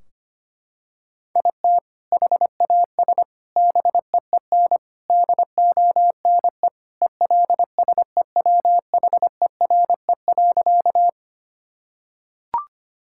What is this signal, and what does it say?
Morse code, heard as a keyed tone, about 25 wpm, 700 Hz. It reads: IT HAS BEEN DONE ELSEWHERE.
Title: Sentences from Top 1600 Words 25wpm